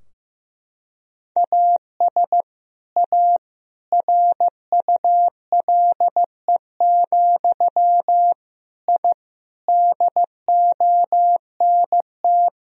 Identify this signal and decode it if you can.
Morse code, heard as a keyed tone, about 15 wpm, 700 Hz. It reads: AS A RULE, I DONT
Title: Sentences from Top 1000 Words - Review Part 2 - 15wpm